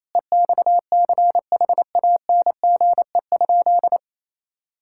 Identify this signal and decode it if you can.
Morse code, heard as a keyed tone, about 28 wpm, 700 Hz. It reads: EXCHANGE?
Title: Sentences from Top 1000 Words - Review Part 6 - 28wpm